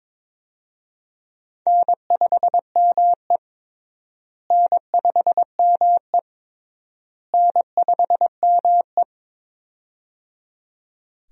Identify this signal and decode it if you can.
Morse code, heard as a keyed tone, about 22 wpm, 700 Hz. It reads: N5ME N5ME N5ME
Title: USA Call Signs - 4 Characters - Repeated 3x - 22wpm